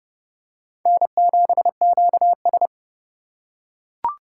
Morse code, heard as a keyed tone, about 26 wpm, 700 Hz. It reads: N7QS